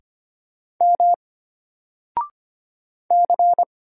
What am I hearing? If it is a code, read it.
Morse code, heard as a keyed tone, about 25 wpm, 700 Hz. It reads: M C
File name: Single-Letter-Number-25wpm